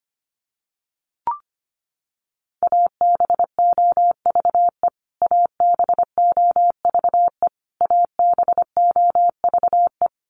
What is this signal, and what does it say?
Morse code, heard as a keyed tone, about 25 wpm, 700 Hz. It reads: ABOVE ABOVE ABOVE
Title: Top 500 Words Repeated 25wpm